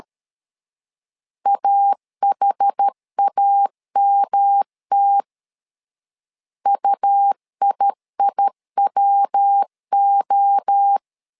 Morse code, heard as a keyed tone, about 13 wpm, 800 Hz. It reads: AHAMT UIIWO
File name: LEZIONI / 4 Q 12wpm 1833